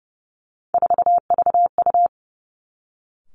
Morse code, heard as a keyed tone, about 31 wpm, 700 Hz. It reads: 4VU
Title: Three Challenging Letter-Number Combinations 30wpm